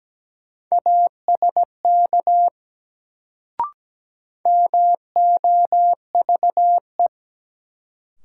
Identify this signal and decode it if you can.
Morse code, heard as a keyed tone, about 17 wpm, 700 Hz. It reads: ASK MOVE
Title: Sentences from Top 200 Words - New Word Drill - 17wpm